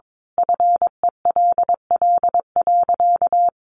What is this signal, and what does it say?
Morse code, heard as a keyed tone, about 22 wpm, 700 Hz. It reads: FELL.
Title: A Sound Of Thunder 22wpm